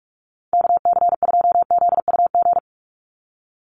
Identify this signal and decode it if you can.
Morse code, heard as a keyed tone, about 44 wpm, 700 Hz. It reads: KC2ZUG
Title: USA Call Signs 45wpm